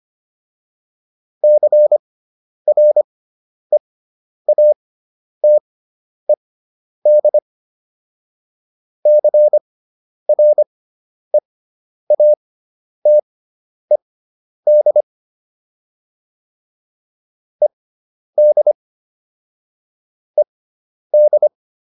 Morse code, heard as a keyed tone, about 25 wpm, spaced as for 10 wpm, 600 Hz. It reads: CREATED CREATED ED ED